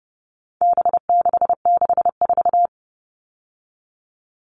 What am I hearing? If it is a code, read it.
Morse code, heard as a keyed tone, about 30 wpm, 700 Hz. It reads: B664
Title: All Cut Numbers - 4-Digit 30wpm